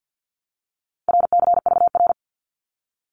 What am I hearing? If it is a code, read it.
Morse code, heard as a keyed tone, about 50 wpm, 700 Hz. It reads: RCVR
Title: Abbreviations Repeated 3x - 50wpm